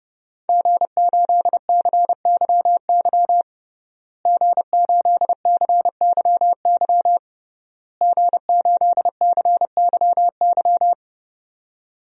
Morse code, heard as a keyed tone, about 30 wpm, 700 Hz. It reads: G8CYY G8CYY G8CYY
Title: DXCC-100 Call Signs - 5 Characters - Repeated 3x - 30wpm